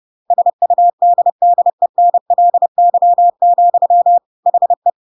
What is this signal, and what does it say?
Morse code, heard as a keyed tone, about 30 wpm, 700 Hz. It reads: SUDDENLY, HE
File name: Pinocchio / Pinocchio ch5 30 30 ik1hge